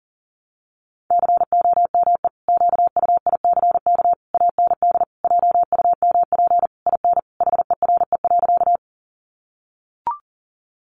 Morse code, heard as a keyed tone, about 37 wpm, 700 Hz. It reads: COME QUICK AND JUMP IN HERE.
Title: Sentences from Top 1300 Words 40wpm